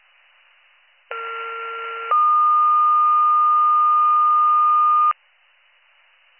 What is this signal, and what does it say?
Two-tone page, heard as a one-second tone, then a three-second tone, about 1370 then 1220 Hz. It is heard from 1.1 to 5.1 s.